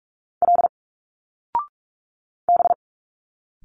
Morse code, heard as a keyed tone, about 44 wpm, 700 Hz. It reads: L B
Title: Single Letters 45wpm